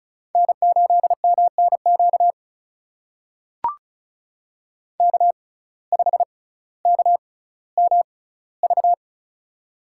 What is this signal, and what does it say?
Morse code, heard as a keyed tone, about 36 wpm, spaced as for 18 wpm, 700 Hz. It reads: N8MNQ K5KMV
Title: USA Call Signs - 5 Characters - 6x Character Spacing - 35wpm